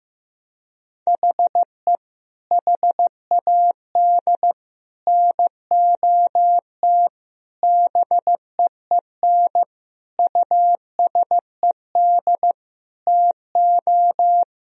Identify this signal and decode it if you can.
Morse code, heard as a keyed tone, about 15 wpm, 700 Hz. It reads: HE HAD NOT BEEN USED TO
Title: Sentences from Top 1000 Words - Review Part 4 - 15wpm